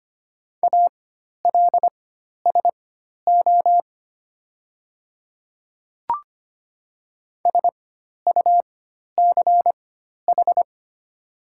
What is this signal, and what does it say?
Morse code, heard as a keyed tone, about 25 wpm, spaced as for 12 wpm, 700 Hz. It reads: ALSO SUCH
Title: 4x Character Spacing - Top 200 Words - 25wpm